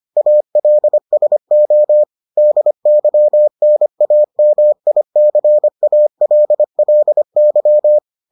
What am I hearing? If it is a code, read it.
Morse code, heard as a keyed tone, about 25 wpm, 600 Hz. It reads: ALSO DYNAMICALLY